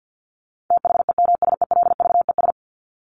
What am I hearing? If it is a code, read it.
Morse code, heard as a keyed tone, about 50 wpm, 700 Hz. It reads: THEMSELVES
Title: Top 751-1000 Words 50wpm